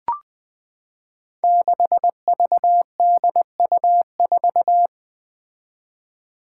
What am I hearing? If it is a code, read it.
Morse code, heard as a keyed tone, about 20 wpm, 700 Hz. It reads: 6VDU4
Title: Five Challenging Letter-Number Combinations 20wpm